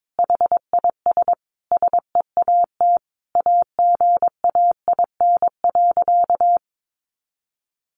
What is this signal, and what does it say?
Morse code, heard as a keyed tone, about 22 wpm, 700 Hz. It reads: HIS SEAT AGAIN.
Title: Sentences from Top 1500 Words 22wpm